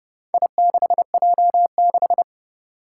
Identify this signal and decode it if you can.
Morse code, heard as a keyed tone, about 30 wpm, 700 Hz. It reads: I6J6
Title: Four Challenging Letter-Number Combinations 30wpm